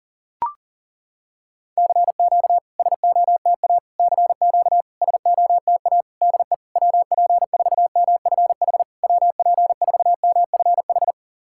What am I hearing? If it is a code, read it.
Morse code, heard as a keyed tone, about 40 wpm, 690 Hz. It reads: CQ SOTA CQ SOTA DE WP4MFH WP4MFH
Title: SOTA QSO Practice 40wpm